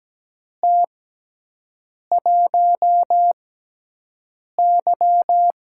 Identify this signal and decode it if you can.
Morse code, heard as a keyed tone, about 17 wpm, spaced as for 6 wpm, 700 Hz. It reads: T1Y
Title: USA Call Signs - 6x Character Spacing - 17wpm